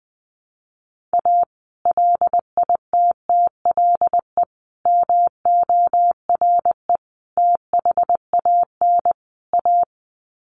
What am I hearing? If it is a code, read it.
Morse code, heard as a keyed tone, about 20 wpm, 700 Hz. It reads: A LITTLE MORE THAN A
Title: Sets of 5 Words Encore2 20wpm